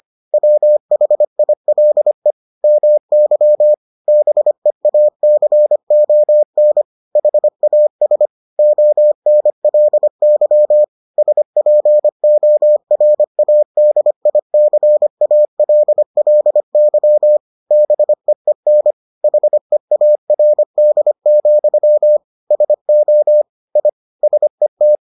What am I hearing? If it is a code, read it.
Morse code, heard as a keyed tone, about 25 wpm, 600 Hz. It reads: WHILE MY BEACON HAS ONLY SPORADICALLY BEEN HEARD, SO I SET